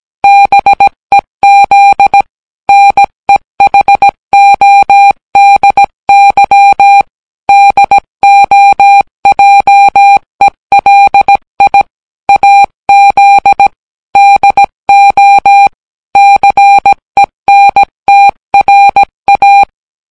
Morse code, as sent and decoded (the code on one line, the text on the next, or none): -... --.. / -. .... --- -.. -.-- / -.. --- .--- . .-.. .. / .- --.. / -.. --- / -.-. -. - .-. .-
BEZ NEHODY DOJELI AZ DO CENTRA